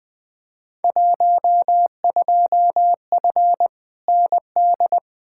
Morse code, heard as a keyed tone, about 20 wpm, 700 Hz. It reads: 12F ND
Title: Field Day Exchanges Encore - 20wpm